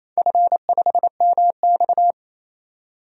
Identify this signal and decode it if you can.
Morse code, heard as a keyed tone, about 28 wpm, 700 Hz. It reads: F5MX
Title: DXCC-100 Call Signs - 4 Characters - 28wpm